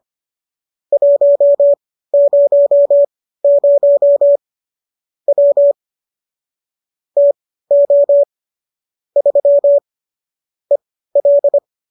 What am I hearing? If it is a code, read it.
Morse code, heard as a keyed tone, about 25 wpm, spaced as for 15 wpm, 570 Hz. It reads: 100 W TO 3 EL